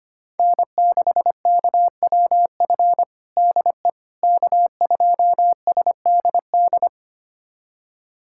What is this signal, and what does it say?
Morse code, heard as a keyed tone, about 25 wpm, 700 Hz. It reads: N6KWF DE K2SDD
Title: USA Callsign DE Callsign - 5 Characters - 25wpm